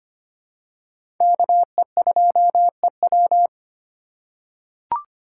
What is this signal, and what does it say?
Morse code, heard as a keyed tone, about 25 wpm, 700 Hz. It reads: KE2EW
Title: USA Call Signs - 5 Characters - 4x Character Spacing - 25wpm